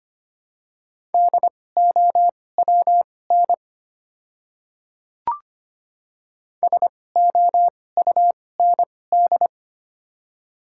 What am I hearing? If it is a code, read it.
Morse code, heard as a keyed tone, about 25 wpm, spaced as for 18 wpm, 700 Hz. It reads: DOWN SOUND
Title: 2x Character Spacing - Top 100 Words - 25wpm